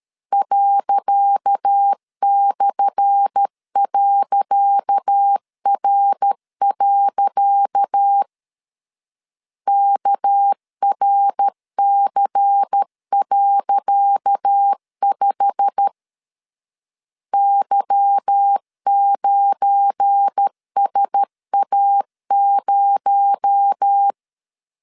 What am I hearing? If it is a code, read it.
Morse code, heard as a keyed tone, about 13 wpm, 800 Hz. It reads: ./.R. KRC.5 Y9SA0
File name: LEZIONI / 10 SIGNS / 10C DOT 12wpm 2451